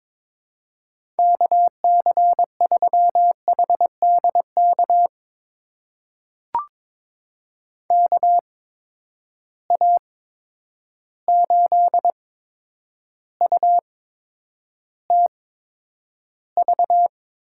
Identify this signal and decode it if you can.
Morse code, heard as a keyed tone, about 22 wpm, spaced as for 9 wpm, 700 Hz. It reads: KC3HDK KA8UTV
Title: USA Call Signs - 6 Characters - 8x Character Spacing - 22wpm